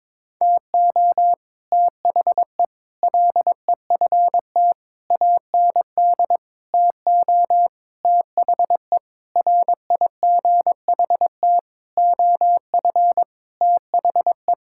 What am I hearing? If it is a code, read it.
Morse code, heard as a keyed tone, about 22 wpm, 700 Hz. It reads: TO THE LEFT AND TO THE RIGHT OF THE
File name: Sentences with 9-Word Phrases - 22wpm